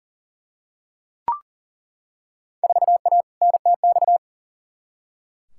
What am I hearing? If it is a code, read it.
Morse code, heard as a keyed tone, about 40 wpm, 700 Hz. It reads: 4A NTX
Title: Field Day Exchanges - 40wpm